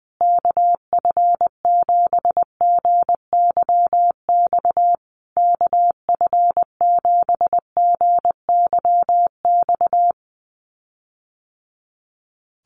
Morse code, heard as a keyed tone, about 20 wpm, 700 Hz. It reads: KF7GYX KF7GYX
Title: USA Call Signs Repeated 3x 20wpm